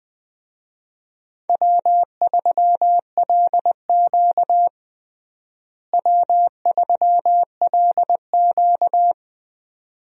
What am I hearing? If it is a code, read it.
Morse code, heard as a keyed tone, about 20 wpm, 700 Hz. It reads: W3LQ W3LQ
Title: USA Call Signs - 4 Characters - Repeated 2x - 20wpm